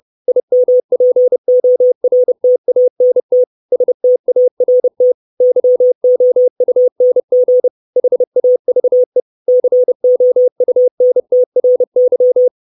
Morse code, heard as a keyed tone, about 30 wpm, 500 Hz. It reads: IMPORTANT START YOUNG HAVE COUNTRY
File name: 632 words / 632 Words 30WPM